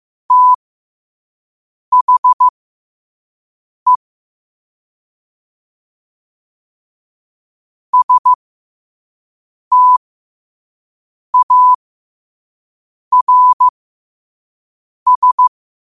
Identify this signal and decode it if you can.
Morse code, heard as a keyed tone, about 15 wpm, spaced as for 5 wpm, 1000 Hz. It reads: THE STARS
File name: qotdfiles / 5wpm